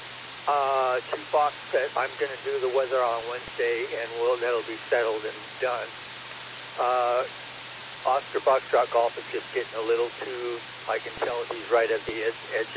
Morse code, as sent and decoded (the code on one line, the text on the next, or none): none